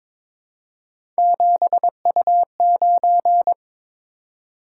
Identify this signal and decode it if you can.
Morse code, heard as a keyed tone, about 22 wpm, 700 Hz. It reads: 7U9
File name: practice / 22wpm / Three Letter-Number Combinations - 22wpm